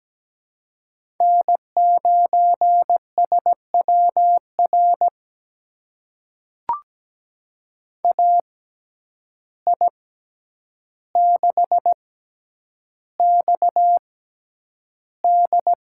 Morse code, heard as a keyed tone, about 17 wpm, spaced as for 9 wpm, 700 Hz. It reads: N9SWR AI6XD
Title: USA Call Signs - 5 Characters - 6x Character Spacing - 17wpm